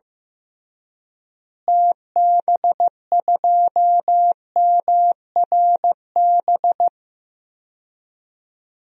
Morse code, heard as a keyed tone, about 15 wpm, 700 Hz. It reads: TB2MRB
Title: DXCC-100 Call Signs - 6 Characters - Repeated 2x - 15wpm